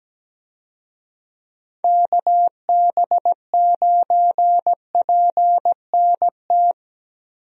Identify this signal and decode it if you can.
Morse code, heard as a keyed tone, about 17 wpm, 700 Hz. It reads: KB9PNT
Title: USA Call Signs - 6 Characters - 17wpm